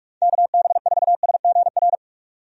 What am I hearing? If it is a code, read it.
Morse code, heard as a keyed tone, about 48 wpm, 700 Hz. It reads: KB4SGR
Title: USA Call Signs - 6 Characters - 45wpm